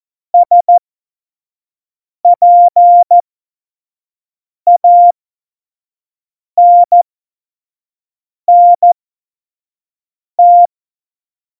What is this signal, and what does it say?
Morse code, heard as a keyed tone, about 14 wpm, spaced as for 5 wpm, 700 Hz. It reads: SPANNT
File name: Morsekurs / Witze / Witz 27. Oktober 2020, 25 F70